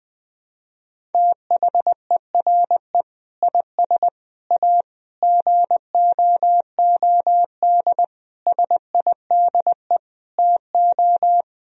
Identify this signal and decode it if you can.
Morse code, heard as a keyed tone, about 20 wpm, 700 Hz. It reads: THERE IS A GOOD SIDE TO
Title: Sentences from Top 1000 Words - Review Part 1 - 20wpm